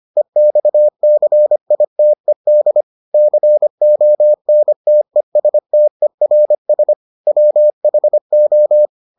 Morse code, heard as a keyed tone, about 25 wpm, 600 Hz. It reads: EXCITED CONTESTERS WHO